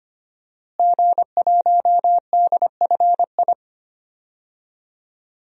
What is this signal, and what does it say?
Morse code, heard as a keyed tone, about 25 wpm, 700 Hz. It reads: G1DFI
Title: DXCC-100 Call Signs - 5 Characters - Repeated 3x - 25wpm